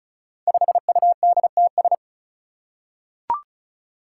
Morse code, heard as a keyed tone, about 36 wpm, 700 Hz. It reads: 5UDTS
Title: Five Challenging Letter-Number Combinations 35wpm